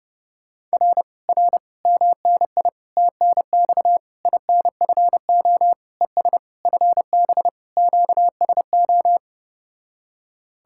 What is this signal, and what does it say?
Morse code, heard as a keyed tone, about 33 wpm, 700 Hz. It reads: R R MNI TNX INFO ES FB QSO